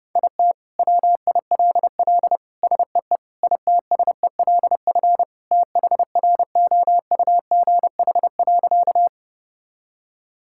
Morse code, heard as a keyed tone, about 30 wpm, 700 Hz. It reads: IT WILL SEE ITSELF THROUGH.
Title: Sentences from Top 1000 Words - Review Part 5 - 30wpm